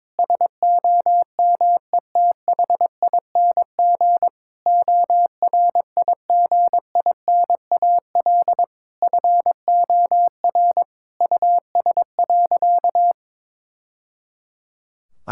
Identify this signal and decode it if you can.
Morse code, heard as a keyed tone, about 22 wpm, 700 Hz. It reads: SOMETHING ORIGINAL FOR US.
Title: Sentences from Top 1000 Words - Review Part 6 - 22wpm